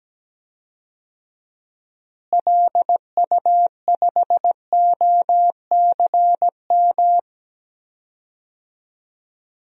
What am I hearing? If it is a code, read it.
Morse code, heard as a keyed tone, about 17 wpm, 700 Hz. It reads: LU5OCM
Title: DXCC-100 Call Signs - 6 Characters - Repeated 2x - 17wpm